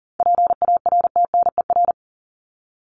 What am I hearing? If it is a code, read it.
Morse code, heard as a keyed tone, about 40 wpm, 700 Hz. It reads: PARTNER
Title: Top 1501-1750 Words 40wpm